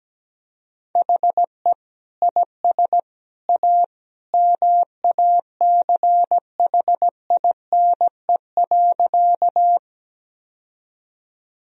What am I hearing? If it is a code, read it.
Morse code, heard as a keyed tone, about 17 wpm, 700 Hz. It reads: HE IS A MACHINE.
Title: Sentences from Top 1000 Words - Review Part 2 - 17wpm